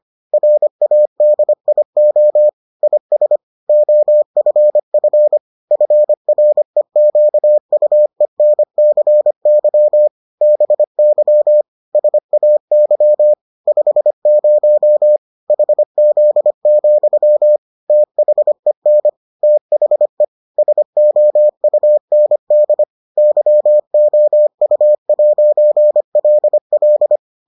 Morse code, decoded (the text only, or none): RADIO IS OFF FREQUENCY BY SAY 50 HZ, THEN THE SOUND YOU'LL